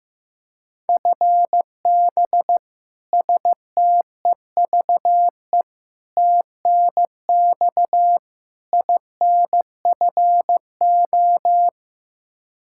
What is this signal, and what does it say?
Morse code, heard as a keyed tone, about 15 wpm, 700 Hz. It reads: FB STEVE TNX INFO
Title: Practice QSO - Exchange 3 - More Info - 15wpm